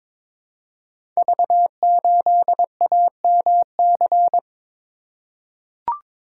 Morse code, heard as a keyed tone, about 23 wpm, 700 Hz. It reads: V8AMC
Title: Code Groups - 5 Letter:Numbers - 22wpm